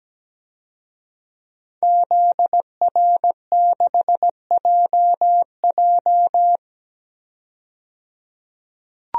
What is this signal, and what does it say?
Morse code, heard as a keyed tone, about 17 wpm, 700 Hz. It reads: ZR6JJ
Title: DXCC-100 Call Signs - 5 Characters - 17wpm